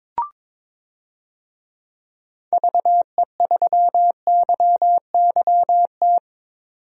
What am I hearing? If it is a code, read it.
Morse code, heard as a keyed tone, about 22 wpm, 700 Hz. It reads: VE3YYT
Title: DXCC-100 Call Signs - 6 Characters - 22wpm